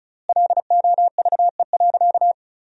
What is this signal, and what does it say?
Morse code, heard as a keyed tone, about 34 wpm, 700 Hz. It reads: LOVE.
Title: 3x Word-Spacing - Sentences from Top 400 Words - 35wpm